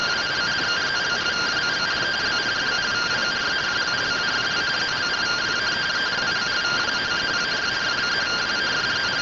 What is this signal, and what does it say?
Two-tone teleprinter signal, about 170 Hz shift.
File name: findings / SAGITTAL 43mm 4mm.gen-07 rtty rx